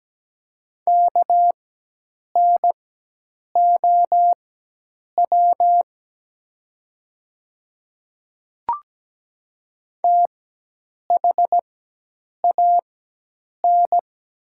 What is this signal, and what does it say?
Morse code, heard as a keyed tone, about 17 wpm, spaced as for 8 wpm, 700 Hz. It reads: KNOW THAN
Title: 4x Character Spacing - Top 100 Words - 17wpm